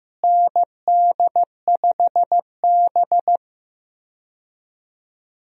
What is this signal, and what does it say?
Morse code, heard as a keyed tone, about 15 wpm, 700 Hz. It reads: ND5B
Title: USA Call Signs - 4 Characters - Repeated 3x - 15wpm